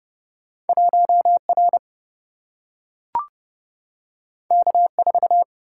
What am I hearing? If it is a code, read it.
Morse code, heard as a keyed tone, about 30 wpm, 700 Hz. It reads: 1 R K 4
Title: Single-Letter Single-Number 30wpm